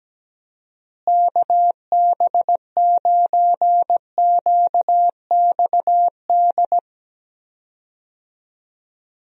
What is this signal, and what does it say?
Morse code, heard as a keyed tone, about 17 wpm, 700 Hz. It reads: KB9QXD
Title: USA Call Signs - 6 Characters - Repeated 3x - 17wpm